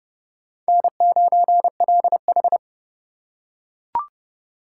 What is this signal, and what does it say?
Morse code, heard as a keyed tone, about 29 wpm, 690 Hz. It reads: N9LH